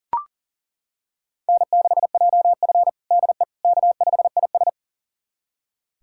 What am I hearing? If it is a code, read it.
Morse code, heard as a keyed tone, about 37 wpm, 700 Hz. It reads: N6JF DE K5IS
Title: USA Callsign DE Callsign - 4 Characters - 40wpm